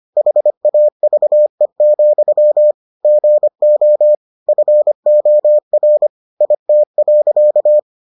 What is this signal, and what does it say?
Morse code, heard as a keyed tone, about 25 wpm, 600 Hz. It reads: HAVE, GO FOR IT.